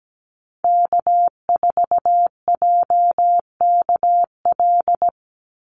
Morse code, heard as a keyed tone, about 17 wpm, 700 Hz. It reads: K4JKL
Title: Call Signs Repeated 2x - 17wpm